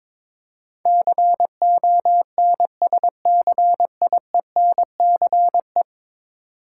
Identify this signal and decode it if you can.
Morse code, heard as a keyed tone, about 21 wpm, 700 Hz. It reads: CONSCIENCE
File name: practice / 22wpm / 10 Letter Words - First 3 Letters Given - 22wpm